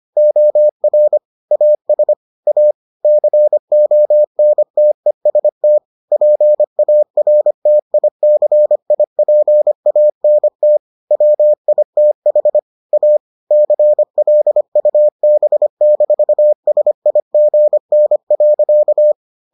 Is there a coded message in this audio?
Yes: Morse code from 0.2 to 19.1 s, heard as a keyed tone, about 25 wpm, 600 Hz. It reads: OR AS A CONTEST PARTICIPANT WITH A CLUB-SIGN.